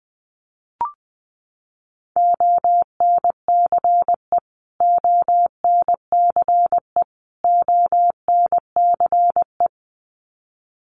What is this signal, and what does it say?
Morse code, heard as a keyed tone, about 20 wpm, 700 Hz. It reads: ONCE ONCE ONCE
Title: Top 300 Words Repeated 20wpm